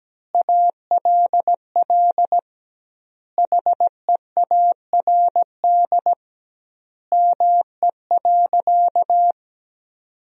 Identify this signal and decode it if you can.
Morse code, heard as a keyed tone, about 17 wpm, 700 Hz. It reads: ALL HEARD ME.
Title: 2x Word-Spacing - Sentences from Top 400 Words - 17wpm